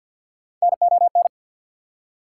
Morse code, heard as a keyed tone, about 54 wpm, 690 Hz. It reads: NON
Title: Prefixes and Suffixes 50wpm